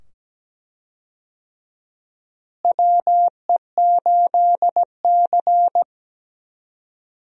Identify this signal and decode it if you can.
Morse code, heard as a keyed tone, about 17 wpm, 700 Hz. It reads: WE8C
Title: USA Call Signs - 4 Characters - Repeated 2x - 17wpm